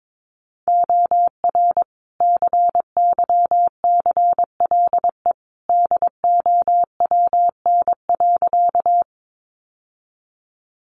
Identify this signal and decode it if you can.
Morse code, heard as a keyed tone, about 22 wpm, 700 Hz. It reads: OR CYCLE DOWN.
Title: Sentences from Top 900 Words 22wpm